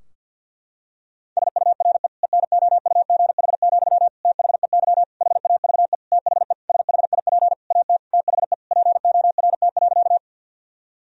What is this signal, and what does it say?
Morse code, heard as a keyed tone, about 53 wpm, 700 Hz. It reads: SURE ENOUGH, THEY HAVE THE SHIP AT THE PORT.
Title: Sentences from Top 1000 Words - Review Part 2 - 50wpm